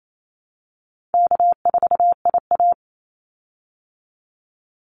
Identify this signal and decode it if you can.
Morse code, heard as a keyed tone, about 28 wpm, 700 Hz. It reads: K4IA
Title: Easy Call Signs Repeated 3x - 28wpm